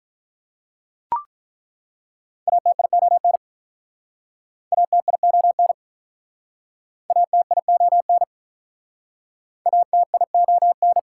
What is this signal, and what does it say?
Morse code, heard as a keyed tone, about 48 wpm, 700 Hz. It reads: ATION ATION ATION ATION